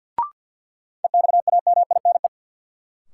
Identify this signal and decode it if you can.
Morse code, heard as a keyed tone, about 53 wpm, 700 Hz. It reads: EXAMINE
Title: Sentences from Top 1600 Words - New Word Drill Alt - 50wpm